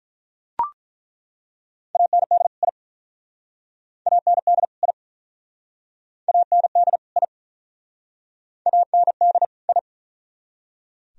Morse code, heard as a keyed tone, about 48 wpm, 700 Hz. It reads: AND I AND I AND I AND I